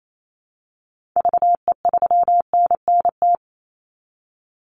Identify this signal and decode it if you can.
Morse code, heard as a keyed tone, about 28 wpm, 700 Hz. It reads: VE3NNT
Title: Call Signs Repeated 3x - 28wpm